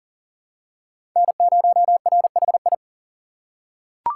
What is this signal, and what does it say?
Morse code, heard as a keyed tone, about 40 wpm, 700 Hz. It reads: N0RHI
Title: USA Call Signs - 5 Characters - 3x Character Spacing - 40wpm